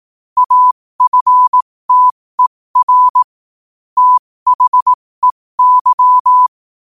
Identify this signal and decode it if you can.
Morse code, heard as a keyed tone, about 18 wpm, spaced as for 15 wpm, 1000 Hz. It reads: AFTER THEY
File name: qotdfiles / 15wpm